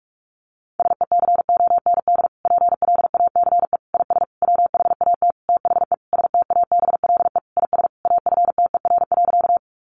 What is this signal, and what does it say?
Morse code, heard as a keyed tone, about 44 wpm, 700 Hz. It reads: SECOND PLACE IS WHAT THE STABLE IS AFTER.
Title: Sentences from Top 1500 Words 45wpm